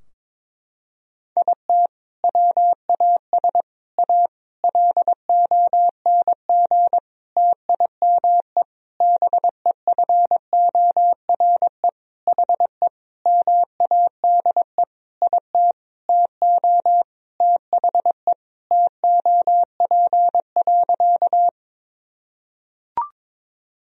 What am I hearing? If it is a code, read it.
Morse code, heard as a keyed tone, about 22 wpm, 700 Hz. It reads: IT WAS A LONG TIME BEFORE HE MADE IT TO THE TOP.
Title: Sentences from Top 1000 Words - Review Part 2 - 22wpm